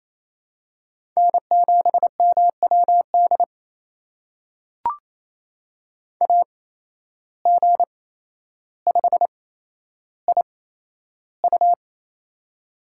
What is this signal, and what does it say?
Morse code, heard as a keyed tone, about 28 wpm, spaced as for 12 wpm, 700 Hz. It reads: N7MWD AG5IU
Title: USA Call Signs - 5 Characters - 8x Character Spacing - 28wpm